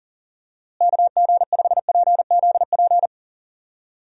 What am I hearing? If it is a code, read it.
Morse code, heard as a keyed tone, about 40 wpm, 700 Hz. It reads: KG5PZP
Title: USA Call Signs - 6 Characters - Repeated 3x - 40wpm